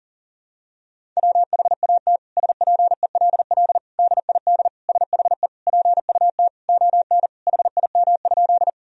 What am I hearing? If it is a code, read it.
Morse code, heard as a keyed tone, about 37 wpm, 700 Hz. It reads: WHAT SPELL DID SHE PUT ON HIM?
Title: Sentences from Top 1000 Words - Review Part 1 - 40wpm